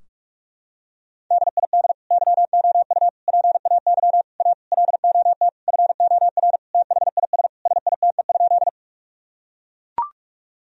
Morse code, heard as a keyed tone, about 41 wpm, 700 Hz. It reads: DID YOU PAY A LOT FOR THIS SITE?
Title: Sentences from Top 1000 Words - Review Part 4 - 45wpm